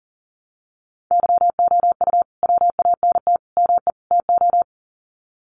Morse code, heard as a keyed tone, about 38 wpm, 700 Hz. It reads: YOU WANT ME TO
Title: Sets of 4 Words 40wpm